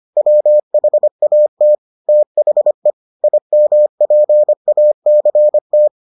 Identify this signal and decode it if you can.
Morse code, heard as a keyed tone, about 25 wpm, 600 Hz. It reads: WHAT THE IMPACT